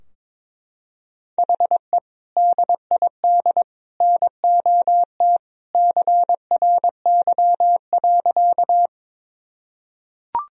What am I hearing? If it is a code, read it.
Morse code, heard as a keyed tone, about 22 wpm, 700 Hz. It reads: HE DID NOT CRY.
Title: Sentences from Top 1000 Words - Review Part 2 - 22wpm